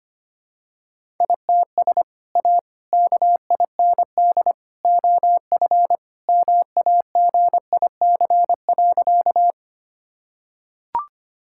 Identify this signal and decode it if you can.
Morse code, heard as a keyed tone, about 25 wpm, 700 Hz. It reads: ITS A KIND OF MAGIC.
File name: practice / 25wpm / Sentences with 3-Word Phrases - 25wpm